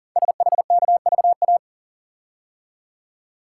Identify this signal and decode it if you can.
Morse code, heard as a keyed tone, about 40 wpm, 700 Hz. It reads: SHKVA
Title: Code Groups - 5 Letters - 40wpm